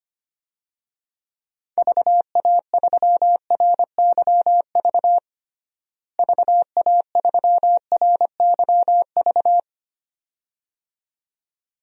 Morse code, heard as a keyed tone, about 25 wpm, 700 Hz. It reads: VA3RYV VA3RYV
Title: DXCC-100 Call Signs - 6 Characters - Repeated 2x - 25wpm